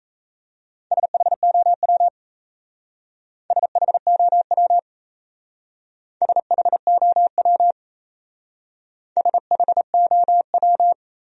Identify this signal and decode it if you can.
Morse code, heard as a keyed tone, about 35 wpm, 700 Hz. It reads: SHOW SHOW SHOW SHOW
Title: Speed-Racing Top 200 Words 28wpm